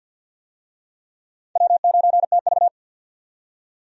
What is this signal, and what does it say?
Morse code, heard as a keyed tone, about 51 wpm, 700 Hz. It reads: W9TV
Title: USA Call Signs - 4 Characters - 50wpm